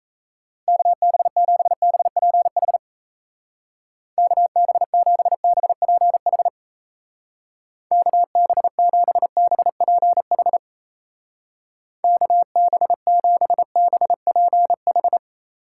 Morse code, heard as a keyed tone, about 35 wpm, 700 Hz. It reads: KB7BPH KB7BPH KB7BPH KB7BPH
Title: Speed Racing - USA 2x3 Call Signs 28wpm